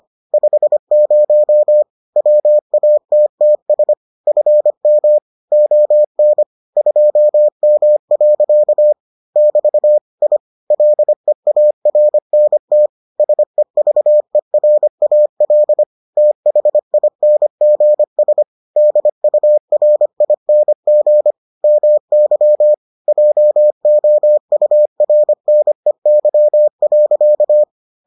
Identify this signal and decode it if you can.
Morse code, heard as a keyed tone, about 25 wpm, 600 Hz. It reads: 50 WATTS FM ON 2M. <BT> I LEARNT SEVERAL THINGS DURING MY JOURNEY.